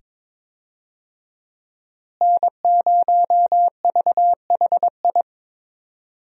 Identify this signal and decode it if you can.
Morse code, heard as a keyed tone, about 22 wpm, 700 Hz. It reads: N0VHI
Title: USA Call Signs - 5 Characters - Repeated 2x - 22wpm